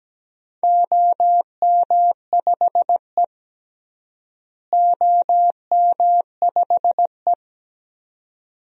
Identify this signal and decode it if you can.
Morse code, heard as a keyed tone, about 17 wpm, 700 Hz. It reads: OM5E OM5E
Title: DXCC-100 Call Signs - 4 Characters - Repeated 2x - 17wpm